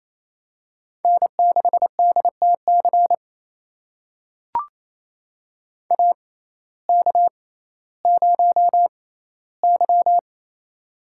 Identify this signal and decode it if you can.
Morse code, heard as a keyed tone, about 28 wpm, spaced as for 15 wpm, 700 Hz. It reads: N6DTC AK0Y
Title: USA Call Signs - 6x Character Spacing - 28wpm